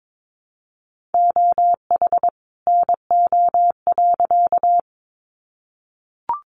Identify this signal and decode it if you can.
Morse code, heard as a keyed tone, about 22 wpm, 700 Hz. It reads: OH NO.
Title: Additional Easy Phrases 22wpm